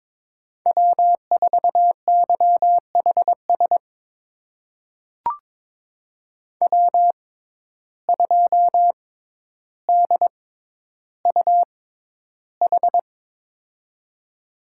Morse code, heard as a keyed tone, about 22 wpm, spaced as for 11 wpm, 700 Hz. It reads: W4YHS W2DUH
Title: USA Call Signs - 5 Characters - 6x Character Spacing - 22wpm